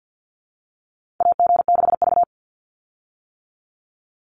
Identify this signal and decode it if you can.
Morse code, heard as a keyed tone, about 49 wpm, 700 Hz. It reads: AG6V
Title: Call Signs - 50wpm